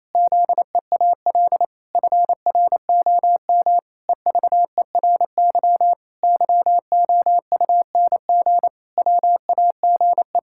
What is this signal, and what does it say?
Morse code, heard as a keyed tone, about 28 wpm, 700 Hz. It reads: ZEAL FROM EVERY YOUNG WAGE
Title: Pangram Challenge v2 - 28wpm